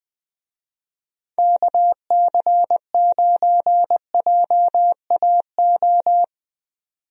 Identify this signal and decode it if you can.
Morse code, heard as a keyed tone, about 20 wpm, 700 Hz. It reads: KC9JAO
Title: USA Call Signs - 6 Characters - 20wpm